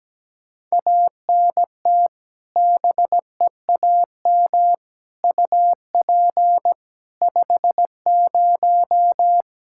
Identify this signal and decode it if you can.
Morse code, heard as a keyed tone, about 17 wpm, 700 Hz. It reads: ANT BEAM UP 50
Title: K1USN SST QSO Practice 17wpm